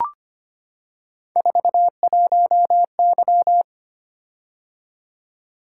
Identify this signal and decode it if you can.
Morse code, heard as a keyed tone, about 25 wpm, 700 Hz. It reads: 41Y
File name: Three Letter-Number Combinations - 25wpm